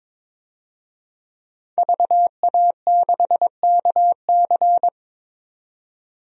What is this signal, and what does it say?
Morse code, heard as a keyed tone, about 22 wpm, 700 Hz. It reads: VA6KC
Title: DXCC-100 Call Signs - 5 Characters - 22wpm